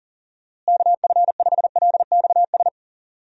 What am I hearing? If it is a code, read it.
Morse code, heard as a keyed tone, about 40 wpm, 700 Hz. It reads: KF5LXS
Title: USA Call Signs - 6 Characters - Repeated 2x - 40wpm